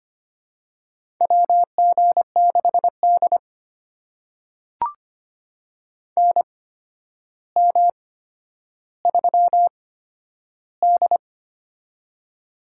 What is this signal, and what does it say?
Morse code, heard as a keyed tone, about 25 wpm, spaced as for 11 wpm, 700 Hz. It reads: WG6D NM3D
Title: USA Call Signs - 4 Characters - 8x Character Spacing - 25wpm